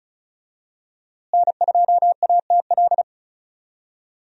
Morse code, heard as a keyed tone, about 36 wpm, 700 Hz. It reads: N2ATL
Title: USA Call Signs - 5 Characters - Repeated 3x - 35wpm